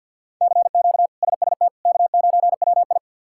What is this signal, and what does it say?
Morse code, heard as a keyed tone, about 51 wpm, 700 Hz. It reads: CQ SST K9WI
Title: K1USN SST QSO Practice - 50wpm